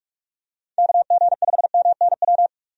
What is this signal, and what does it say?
Morse code, heard as a keyed tone, about 46 wpm, 700 Hz. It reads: KG5MNW
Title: USA Call Signs - 6 Characters - 45wpm